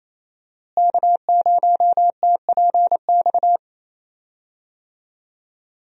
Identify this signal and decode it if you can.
Morse code, heard as a keyed tone, about 27 wpm, 700 Hz. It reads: K0TPX
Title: USA Call Signs - 5 Characters - Repeated 2x - 28wpm